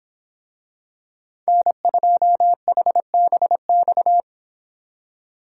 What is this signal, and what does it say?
Morse code, heard as a keyed tone, about 24 wpm, 700 Hz. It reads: N2HBX